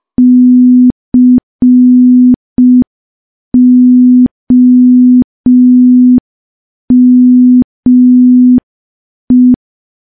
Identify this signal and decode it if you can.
Morse code, heard as a keyed tone, about 5 wpm, 250 Hz. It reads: COME
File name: morse / morse-training-05WPM